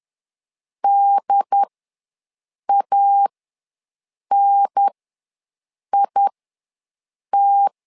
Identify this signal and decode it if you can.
Morse code, heard as a keyed tone, about 10 wpm, spaced as for 6 wpm, 800 Hz. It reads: DANIT
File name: 2 UHDN ws 10wpm